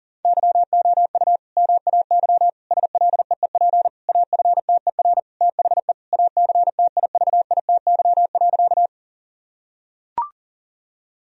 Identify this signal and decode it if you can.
Morse code, heard as a keyed tone, about 40 wpm, 700 Hz. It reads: YOU MAY SLEEP AFTER THE ACTIVITY.
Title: Sentences from Top 1000 Words - Review Part 4 - 40wpm